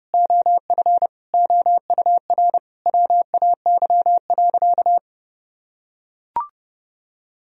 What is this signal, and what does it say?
Morse code, heard as a keyed tone, about 32 wpm, 700 Hz. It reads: OF OUR WAY.
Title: Sentences from Top 1000 Words - Review Part 1 - 30wpm